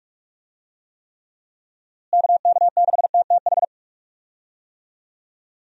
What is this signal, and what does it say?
Morse code, heard as a keyed tone, about 41 wpm, 700 Hz. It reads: KK6TTH
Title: USA Call Signs - 6 Characters - Repeated 2x - 45wpm